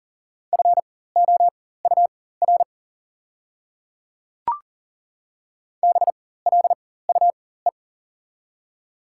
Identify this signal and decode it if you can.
Morse code, heard as a keyed tone, about 40 wpm, spaced as for 19 wpm, 700 Hz. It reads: FOUR BLUE
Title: 4x Character Spacing - Top 500 Words - 40wpm